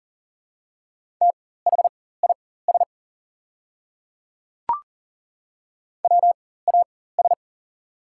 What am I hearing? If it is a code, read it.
Morse code, heard as a keyed tone, about 40 wpm, spaced as for 19 wpm, 690 Hz. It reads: THIS WAS
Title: 4x Character Spacing - Top 500 Words - 40wpm